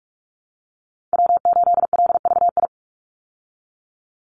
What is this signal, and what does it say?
Morse code, heard as a keyed tone, about 44 wpm, 700 Hz. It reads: W8LVI
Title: Call Signs Repeated 2x - 45wpm